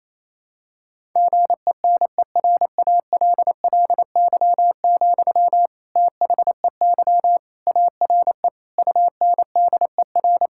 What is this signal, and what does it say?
Morse code, heard as a keyed tone, about 28 wpm, 700 Hz. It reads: GENERALLY, THEY ARE UNDER
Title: Sentences from Top 1000 Words - Review Part 6 - 28wpm